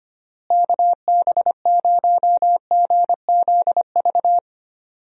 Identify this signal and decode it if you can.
Morse code, heard as a keyed tone, about 25 wpm, 700 Hz. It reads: KB0GZV
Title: USA Call Signs - 6 Characters - Repeated 3x - 25wpm